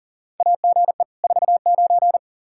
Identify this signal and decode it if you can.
Morse code, heard as a keyed tone, about 40 wpm, 700 Hz. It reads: AGE 49
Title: K1USN SST QSO Practice 40wpm